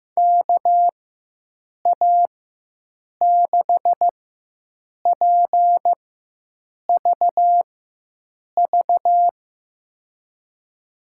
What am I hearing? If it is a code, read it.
Morse code, heard as a keyed tone, about 15 wpm, spaced as for 7 wpm, 700 Hz. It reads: KA6PVV
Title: USA Call Signs - 6 Characters - 4x Character Spacing - 15wpm